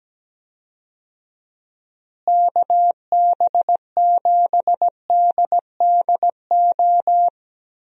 Morse code, heard as a keyed tone, about 17 wpm, 700 Hz. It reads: KB7DDO
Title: USA Call Signs - 6 Characters - Repeated 2x - 17wpm